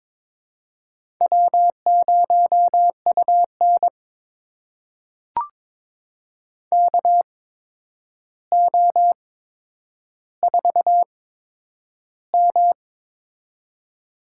Morse code, heard as a keyed tone, about 22 wpm, spaced as for 9 wpm, 700 Hz. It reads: W0UN KO4M
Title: USA Call Signs - 4 Characters - 8x Character Spacing - 22wpm